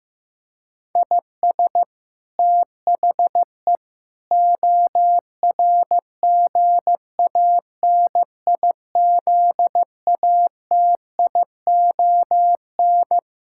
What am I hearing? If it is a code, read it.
Morse code, heard as a keyed tone, about 15 wpm, 700 Hz. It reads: IS THE ORGANIZATION
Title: Sentences from Top 1000 Words - Review Part 5 - 15wpm